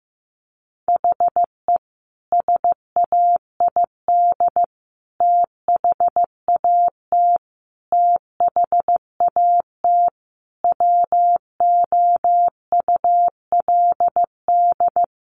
Morse code, heard as a keyed tone, about 15 wpm, 700 Hz. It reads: HE SAID THAT THAT WOULD
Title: Sentences from Top 200 Words 15wpm